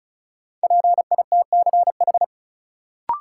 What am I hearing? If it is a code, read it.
Morse code, heard as a keyed tone, about 36 wpm, 700 Hz. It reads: PITCH
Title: Sentences from Top 1300 Words - New Word Drill - 35wpm